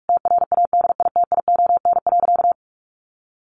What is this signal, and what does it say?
Morse code, heard as a keyed tone, about 44 wpm, 700 Hz. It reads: TRADITION.
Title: Sentences from Top 1500 Words 45wpm